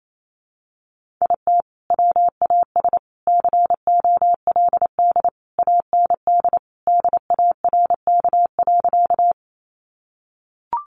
Morse code, heard as a keyed tone, about 28 wpm, 700 Hz. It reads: IT WAS COLD AND DARK.
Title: Sentences from Top 500 Words 28wpm